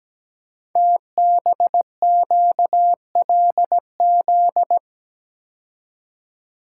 Morse code, heard as a keyed tone, about 17 wpm, 700 Hz. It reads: TBQLZ
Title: Code Groups - 5 Letter:Numbers - 17wpm